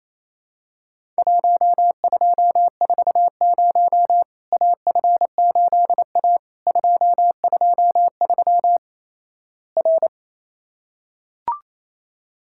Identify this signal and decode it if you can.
Morse code, heard as a keyed tone, about 28 wpm, 700 Hz. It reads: 1240 AF8A 223 R
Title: Worked All Europe DX Contest - QTC Exchange - 28wpm